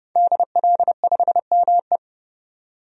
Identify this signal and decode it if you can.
Morse code, heard as a keyed tone, about 31 wpm, 700 Hz. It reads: DL5ME
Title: DXCC-100 Call Signs - 5 Characters - 30wpm